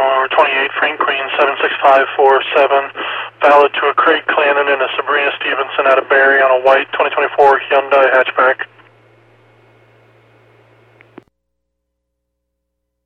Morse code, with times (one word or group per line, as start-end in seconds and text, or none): none